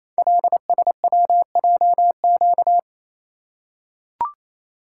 Morse code, heard as a keyed tone, about 28 wpm, 700 Hz. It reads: LSWJQ
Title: Code Groups - 5 Letter:Numbers - 28wpm